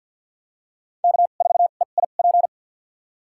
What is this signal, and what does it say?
Morse code, heard as a keyed tone, about 51 wpm, spaced as for 36 wpm, 700 Hz. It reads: K4EIP
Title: USA Call Signs - 2x Character Spacing - 50wpm